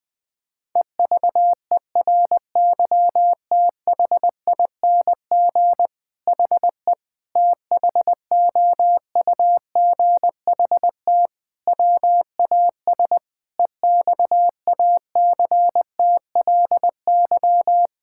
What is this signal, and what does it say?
Morse code, heard as a keyed tone, about 20 wpm, 700 Hz. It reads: EVERYTHING HE THOUGHT WAS EXACTLY
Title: Sentences from Top 1000 Words - Review Part 6 - 20wpm